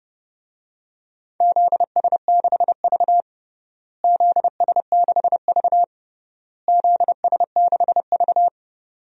Morse code, heard as a keyed tone, about 32 wpm, 700 Hz. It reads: ZS6V ZS6V ZS6V
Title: DXCC-100 Call Signs - 4 Characters - Repeated 3x - 30wpm